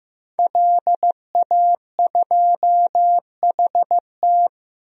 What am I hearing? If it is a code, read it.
Morse code, heard as a keyed tone, about 15 wpm, 700 Hz. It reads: LA2HT
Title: DXCC-100 Call Signs - 5 Characters - 15wpm